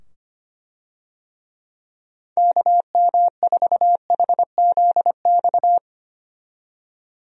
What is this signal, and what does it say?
Morse code, heard as a keyed tone, about 25 wpm, 700 Hz. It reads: KM4HZX
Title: USA Call Signs - 6 Characters - 25wpm